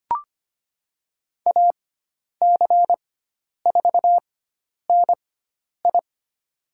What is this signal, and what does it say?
Morse code, heard as a keyed tone, about 25 wpm, spaced as for 10 wpm, 700 Hz. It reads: AC4NI
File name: Lesson 16g - Letter c - Call Signs 25wpm